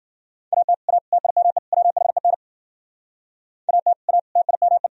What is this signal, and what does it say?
Morse code, heard as a keyed tone, about 51 wpm, 700 Hz. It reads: AT A TIME WHEN AT A TIME